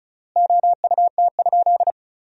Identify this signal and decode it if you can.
Morse code, heard as a keyed tone, about 36 wpm, 700 Hz. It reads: OUT?
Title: 3x Word-Spacing - Sentences from Top 100 Words - 35wpm